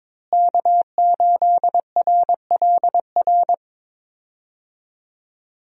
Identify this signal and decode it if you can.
Morse code, heard as a keyed tone, about 22 wpm, 700 Hz. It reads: K8RLR
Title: USA Call Signs - 5 Characters - Repeated 3x - 22wpm